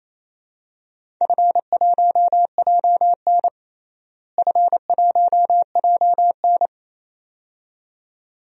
Morse code, heard as a keyed tone, about 28 wpm, 700 Hz. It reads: F1JN F1JN
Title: DXCC-100 Call Signs - 4 Characters - Repeated 2x - 28wpm